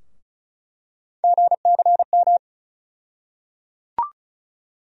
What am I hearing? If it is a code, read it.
Morse code, heard as a keyed tone, about 36 wpm, 700 Hz. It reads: GCM
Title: USA Call Sign Postfixes - 3 Letters 35wpm